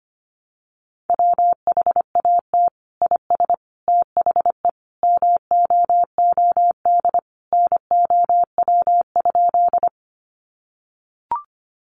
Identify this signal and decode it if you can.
Morse code, heard as a keyed tone, about 25 wpm, 700 Hz. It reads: WHAT IS THE MOOD NOW?
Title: Sentences from Top 1100 Words 25wpm